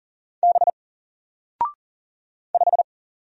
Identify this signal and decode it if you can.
Morse code, heard as a keyed tone, about 40 wpm, 700 Hz. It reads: B 5
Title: Single Character (Letter:Number:Punctuation:Prosign) 40wpm